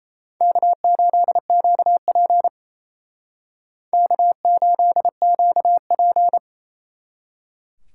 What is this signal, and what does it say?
Morse code, heard as a keyed tone, about 32 wpm, 700 Hz. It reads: K8QP K8QP
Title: Speed Racing - USA 1x2 Call Signs 28wpm